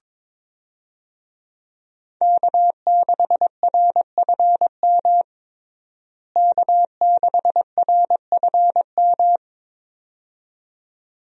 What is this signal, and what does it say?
Morse code, heard as a keyed tone, about 21 wpm, 700 Hz. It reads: K6RFM K6RFM
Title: USA Call Signs - 5 Characters - Repeated 2x - 22wpm